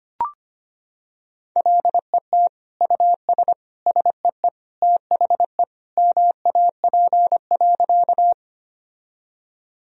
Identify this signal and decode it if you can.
Morse code, heard as a keyed tone, about 25 wpm, 700 Hz. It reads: LET US SEE THE MAP.
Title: Sentences from Top 1000 Words - Review Part 2 - 25wpm